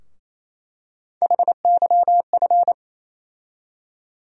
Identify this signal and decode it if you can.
Morse code, heard as a keyed tone, about 28 wpm, 700 Hz. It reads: HYF
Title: USA Call Sign Postfixes - 3 Letters 28wpm